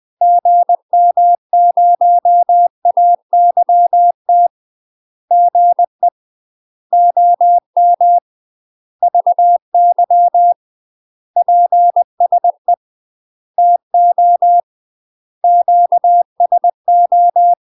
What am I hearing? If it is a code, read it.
Morse code, heard as a keyed tone, about 20 wpm, 700 Hz. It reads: GM0AYT GE OM VY PSE TO QSO
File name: UK MASTER QSOs TRAINING / 20WPM / UK MASTER QSOs 0040